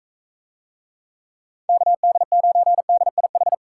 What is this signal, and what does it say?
Morse code, heard as a keyed tone, about 42 wpm, 700 Hz. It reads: KD9DIH